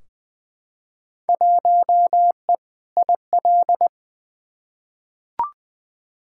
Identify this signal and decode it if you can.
Morse code, heard as a keyed tone, about 21 wpm, 700 Hz. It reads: 1E IL
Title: Field Day Exchanges - 20wpm